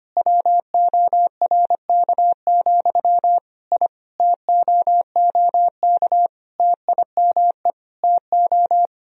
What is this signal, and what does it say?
Morse code, heard as a keyed tone, about 25 wpm, 690 Hz. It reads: WORK, I TOOK TIME TO
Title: Sentences from Top 1000 Words - Review Part 4 - 25wpm